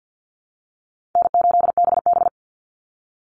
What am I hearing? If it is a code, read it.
Morse code, heard as a keyed tone, about 50 wpm, 700 Hz. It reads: N8BB
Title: NA SOTA Chaser Callsigns 50wpm